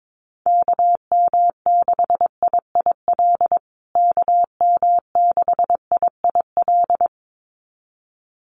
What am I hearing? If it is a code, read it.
Morse code, heard as a keyed tone, about 22 wpm, 700 Hz. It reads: KM6IIL KM6IIL
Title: USA Call Signs Repeated 2x 22wpm